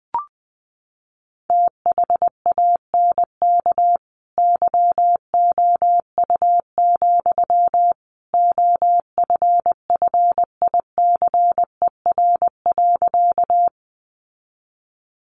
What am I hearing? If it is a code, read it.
Morse code, heard as a keyed tone, about 20 wpm, 700 Hz. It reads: THANK YOU, OFFICER.
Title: Sentences from Top 1200 Words 20wpm